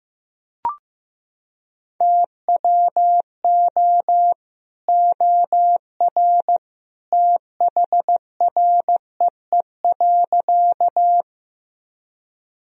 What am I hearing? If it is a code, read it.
Morse code, heard as a keyed tone, about 15 wpm, 700 Hz. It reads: TWO OR THREE.
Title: Sentences from Top 1000 Words - Review Part 1 - 15wpm